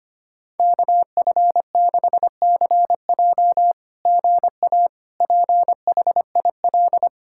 Morse code, heard as a keyed tone, about 25 wpm, 700 Hz. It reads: KF6CJ GA PHIL
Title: SOTA QSO Practice 25wpm